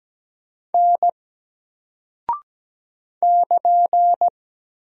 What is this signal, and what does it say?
Morse code, heard as a keyed tone, about 17 wpm, 700 Hz. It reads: N <KN>
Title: Single Character (Letter:Number:Punctuation:Prosign) 17wpm